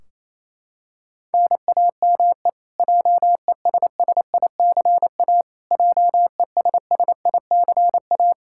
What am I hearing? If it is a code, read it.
Morse code, heard as a keyed tone, about 28 wpm, 700 Hz. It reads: NAME JESSICA JESSICA
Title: Practice QSO - Exchange 1 - Beginning - 28wpm